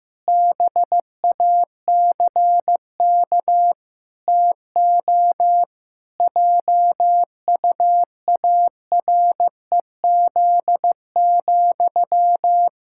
Morse code, heard as a keyed tone, about 15 wpm, 700 Hz. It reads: BACK TO JUAREZ,
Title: Pangram Challenge v2 - 15wpm